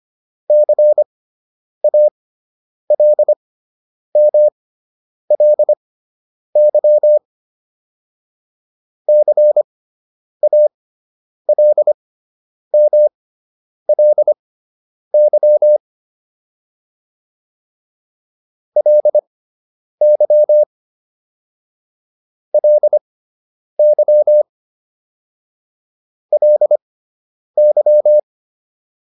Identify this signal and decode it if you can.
Morse code, heard as a keyed tone, about 25 wpm, spaced as for 9 wpm, 600 Hz. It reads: CALMLY CALMLY LY LY LY